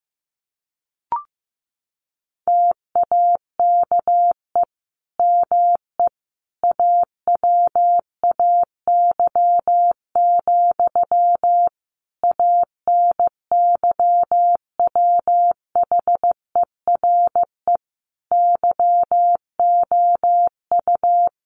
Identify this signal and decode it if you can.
Morse code, heard as a keyed tone, about 15 wpm, 700 Hz. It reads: TAKE ME AWAY, ANYWHERE YOU
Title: Sentences from Top 1400 Words 15wpm